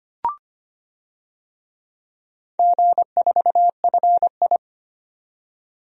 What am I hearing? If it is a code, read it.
Morse code, heard as a keyed tone, about 25 wpm, 690 Hz. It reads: G4FI
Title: DXCC-100 Call Signs - 4 Characters - 25wpm